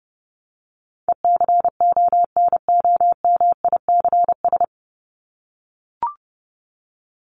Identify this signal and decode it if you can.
Morse code, heard as a keyed tone, about 30 wpm, 700 Hz. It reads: ECONOMICS
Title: Top 501-750 Words 30wpm